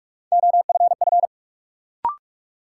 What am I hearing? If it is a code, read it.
Morse code, heard as a keyed tone, about 46 wpm, 700 Hz. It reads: OFF
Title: Sets of 2 Words - New Word Drill - 45wpm